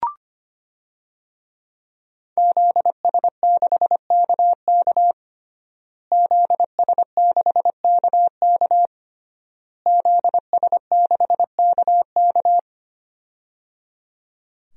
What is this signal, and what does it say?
Morse code, heard as a keyed tone, about 25 wpm, 700 Hz. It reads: ZS6KK ZS6KK ZS6KK
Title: DXCC-100 Call Signs - 5 Characters - Repeated 3x - 25wpm